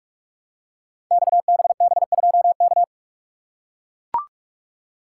Morse code, heard as a keyed tone, about 44 wpm, 700 Hz. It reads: XBB2K
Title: Code Groups - 5 Letter:Numbers - 45wpm